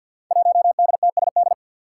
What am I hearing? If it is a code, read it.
Morse code, heard as a keyed tone, about 47 wpm, 700 Hz. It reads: 1DTSD
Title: Five Challenging Letter-Number Combinations 50wpm